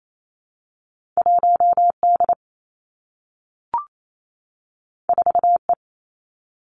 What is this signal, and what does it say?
Morse code, heard as a keyed tone, about 28 wpm, 700 Hz. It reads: 1 D 4 E
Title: Two Challenging Letter-Number Combinations 28wpm